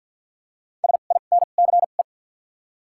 Morse code, heard as a keyed tone, about 53 wpm, spaced as for 37 wpm, 700 Hz. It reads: SINCE
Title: 2x Character Spacing - Top 300 Words - 50wpm